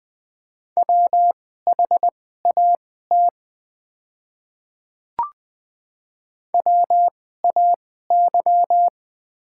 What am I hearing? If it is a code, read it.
Morse code, heard as a keyed tone, about 20 wpm, spaced as for 14 wpm, 690 Hz. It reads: WHAT WAY
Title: 2x Character Spacing - Top 200 Words - 20wpm